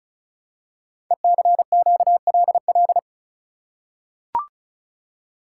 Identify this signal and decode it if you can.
Morse code, heard as a keyed tone, about 36 wpm, 690 Hz. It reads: ECQLL
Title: Code Groups - 5 Letters - 35wpm